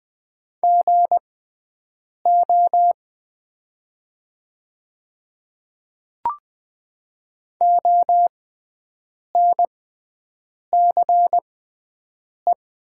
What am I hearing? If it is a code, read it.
Morse code, heard as a keyed tone, about 20 wpm, spaced as for 7 wpm, 700 Hz. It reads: GO ONCE
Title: 6x Character Spacing - Top 300 Words - 20wpm